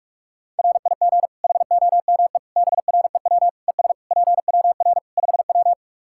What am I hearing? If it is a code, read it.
Morse code, heard as a keyed tone, about 41 wpm, 700 Hz. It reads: RIG HOME BREW ES PWR 5W
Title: K1USN SST QSO Practice 45wpm